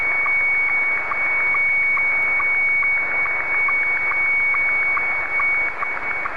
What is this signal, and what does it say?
SSTV (slow-scan television), Scottie 1.